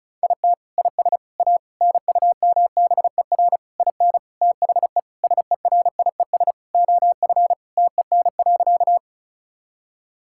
Morse code, heard as a keyed tone, about 34 wpm, 700 Hz. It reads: IT IS A NUMBER IN THE SERIES OF TEN.
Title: Sentences from Top 1000 Words - Review Part 5 - 35wpm